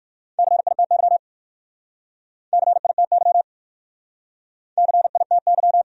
Morse code, heard as a keyed tone, about 52 wpm, 700 Hz. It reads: CITY CITY CITY
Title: Speed-Racing Top 300 Words 45wpm